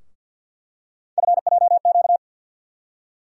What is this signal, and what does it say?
Morse code, heard as a keyed tone, about 49 wpm, 700 Hz. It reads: FJQ
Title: USA Call Sign Postfixes - 3 Letters 50wpm